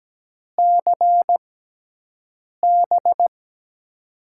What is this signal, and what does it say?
Morse code, heard as a keyed tone, about 17 wpm, spaced as for 6 wpm, 700 Hz. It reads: CB
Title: USA Call Signs - 6 Characters - 6x Character Spacing - 17wpm